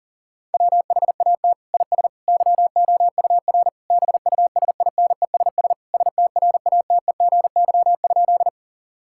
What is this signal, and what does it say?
Morse code, heard as a keyed tone, about 37 wpm, 700 Hz. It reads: WHAT IS YOUR BUSINESS STRATEGY?
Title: Sentences from Top 1000 Words - Review Part 5 - 40wpm